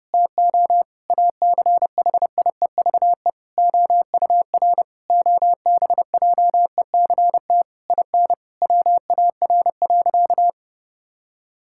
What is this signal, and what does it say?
Morse code, heard as a keyed tone, about 30 wpm, 700 Hz. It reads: TO ACHIEVE OUR OBJECT IN WAR.
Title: Sentences from Top 1000 Words - Review Part 6 - 30wpm